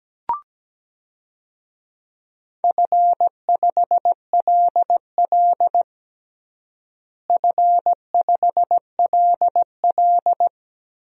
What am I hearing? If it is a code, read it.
Morse code, heard as a keyed tone, about 17 wpm, 700 Hz. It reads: F5LL F5LL
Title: DXCC-100 Call Signs - 4 Characters - Repeated 3x - 17wpm